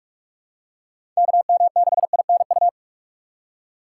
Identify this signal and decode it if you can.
Morse code, heard as a keyed tone, about 42 wpm, 700 Hz. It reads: KM6INU
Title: USA Call Signs - 6 Characters - 2x Character Spacing - 45wpm